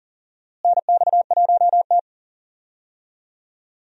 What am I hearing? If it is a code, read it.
Morse code, heard as a keyed tone, about 40 wpm, 700 Hz. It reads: NX1T
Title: USA Call Signs - 4 Characters - Repeated 3x - 40wpm